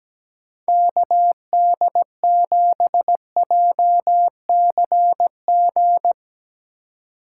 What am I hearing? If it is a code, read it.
Morse code, heard as a keyed tone, about 17 wpm, 700 Hz. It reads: KD7JCG
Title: USA Call Signs - 2x Character Spacing - 17wpm